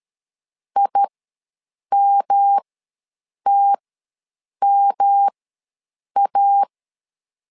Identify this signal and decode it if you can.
Morse code, heard as a keyed tone, about 13 wpm, spaced as for 7 wpm, 800 Hz. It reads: IMTMA
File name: LEZIONI / 3 MWRO / 3 M ws 12wpm 1745